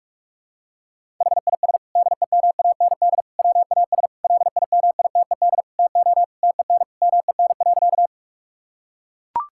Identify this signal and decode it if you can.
Morse code, heard as a keyed tone, about 41 wpm, 700 Hz. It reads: HIS DEMAND WAS LIMITED TO TEN MEN.
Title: Sentences from Top 1000 Words - Review Part 6 - 45wpm